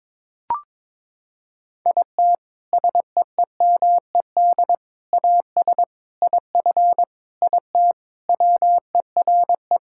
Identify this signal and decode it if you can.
Morse code, heard as a keyed tone, about 22 wpm, 700 Hz. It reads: IT SEEMED AS IF IT WERE
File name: practice / 22wpm / Sentences with 4-Word Phrases - 22wpm